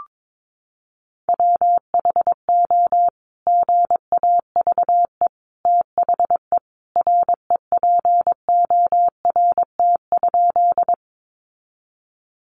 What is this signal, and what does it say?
Morse code, heard as a keyed tone, about 22 wpm, 700 Hz. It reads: WHO GAVE THE REPORT?
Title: Sentences from Top 900 Words 22wpm